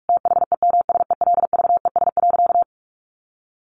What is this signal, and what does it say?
Morse code, heard as a keyed tone, about 44 wpm, 700 Hz. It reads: THEMSELVES.
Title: Sentences from Top 1100 Words 45wpm